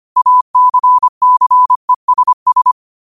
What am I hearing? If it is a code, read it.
Morse code, heard as a keyed tone, about 25 wpm, 1000 Hz. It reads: ACCESS
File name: qotdfiles / 25wpm